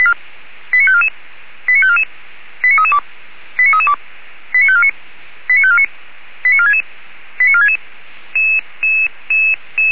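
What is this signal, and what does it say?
Five-tone selective call (ZVEI), heard as quick runs of five tones, tones of about 70 ms. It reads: 89544 89544 89332 89332 89549 89549 89466 89466
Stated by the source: ZVEI five-tone call received via SDR, fire brigade Orlamünde (feuerwehr-orlamuende).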